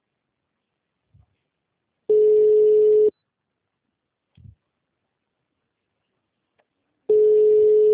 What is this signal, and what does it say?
Ringback tone (Europe, 425 Hz).
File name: out- -2020- - - .35389